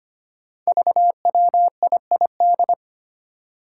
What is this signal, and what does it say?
Morse code, heard as a keyed tone, about 25 wpm, 700 Hz. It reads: VWIID
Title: Five Challenging Letter-Number Combinations 25wpm